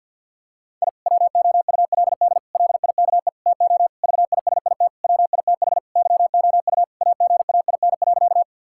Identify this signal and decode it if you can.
Morse code, heard as a keyed tone, about 50 wpm, 700 Hz. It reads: I WOULD LIKE TO VISIT WITH YOU AGAIN.
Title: Sentences from Top 1000 Words - Review Part 6 - 50wpm